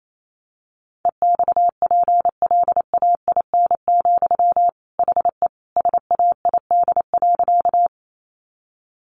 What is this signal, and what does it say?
Morse code, heard as a keyed tone, about 28 wpm, 700 Hz. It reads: EXPLAIN, HE SAID.
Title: Sentences from Top 1100 Words 28wpm